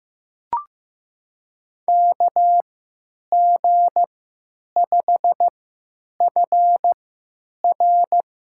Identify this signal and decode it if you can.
Morse code, heard as a keyed tone, about 15 wpm, spaced as for 8 wpm, 700 Hz. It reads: KG5FR
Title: USA Call Signs - 3x Character Spacing - 15wpm